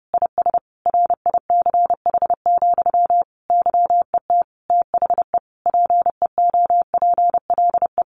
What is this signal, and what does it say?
Morse code, heard as a keyed tone, about 30 wpm, 700 Hz. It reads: IS RICH, YET THE PEOPLE
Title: Sentences from Top 1100 Words 30wpm